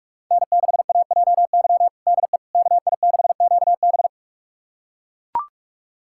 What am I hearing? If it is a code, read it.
Morse code, heard as a keyed tone, about 44 wpm, 700 Hz. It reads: N6AJY DE KI6QB
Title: USA Callsign DE Callsign - 5 Characters - 45wpm